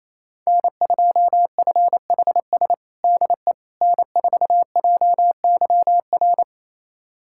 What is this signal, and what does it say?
Morse code, heard as a keyed tone, about 28 wpm, 700 Hz. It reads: N2FHS DE N4JYR
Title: USA Callsign DE Callsign - 5 Characters - 28wpm